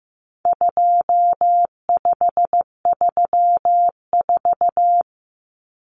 Morse code, heard as a keyed tone, about 15 wpm, 700 Hz. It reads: 2534
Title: Numbers - 4-Digit 15wpm